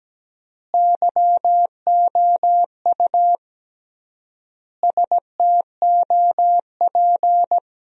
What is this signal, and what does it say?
Morse code, heard as a keyed tone, about 17 wpm, 700 Hz. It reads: YOU STOP
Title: 3x Word-Spacing - Sentences from Top 300 Words - 17wpm